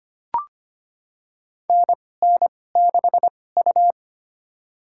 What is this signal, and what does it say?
Morse code, heard as a keyed tone, about 25 wpm, spaced as for 18 wpm, 700 Hz. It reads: NN6U
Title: USA Call Signs - 2x Character Spacing - 25wpm